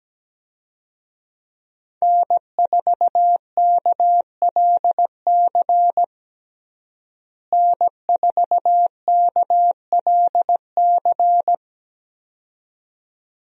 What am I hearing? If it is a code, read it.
Morse code, heard as a keyed tone, about 17 wpm, 700 Hz. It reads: N4KLC N4KLC
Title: USA Call Signs - 5 Characters - Repeated 2x - 17wpm